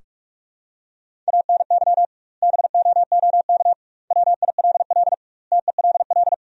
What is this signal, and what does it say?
Morse code, heard as a keyed tone, about 45 wpm, 700 Hz. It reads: ANY BOOK WILL TELL
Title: 2x Word-Spacing - Sentences from Top 300 Words - 45wpm